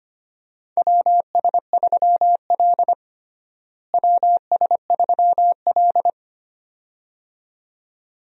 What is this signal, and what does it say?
Morse code, heard as a keyed tone, about 25 wpm, 700 Hz. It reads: WS3L WS3L
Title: USA Call Signs - 4 Characters - Repeated 2x - 25wpm